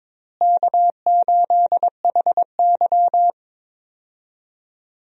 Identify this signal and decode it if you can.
Morse code, heard as a keyed tone, about 22 wpm, 700 Hz. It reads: K8HY
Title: USA Call Signs - 4 Characters - Repeated 2x - 22wpm